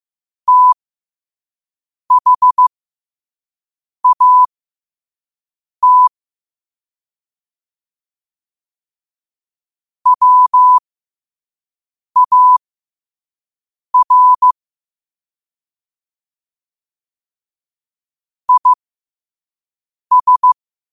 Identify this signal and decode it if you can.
Morse code, heard as a keyed tone, about 15 wpm, spaced as for 5 wpm, 1000 Hz. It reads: THAT WAR IS
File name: qotdfiles / 5wpm